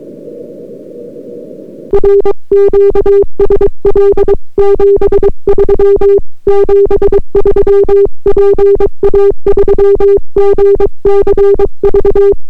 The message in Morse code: RQSL 73 73 PA3GCV